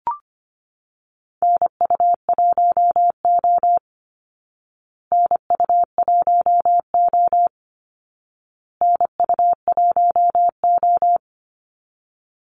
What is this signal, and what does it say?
Morse code, heard as a keyed tone, about 25 wpm, 700 Hz. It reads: NU1O NU1O NU1O
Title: Call Signs Repeated 2x - 25wpm